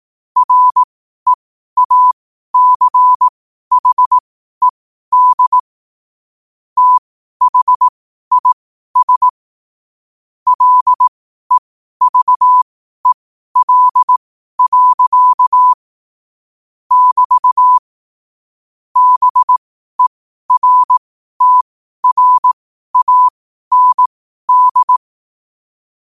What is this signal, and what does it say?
Morse code, heard as a keyed tone, about 18 wpm, spaced as for 12 wpm, 1000 Hz. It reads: REACHED THIS LEVEL. <BT> BERTRAND
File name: qotdfiles / 12wpm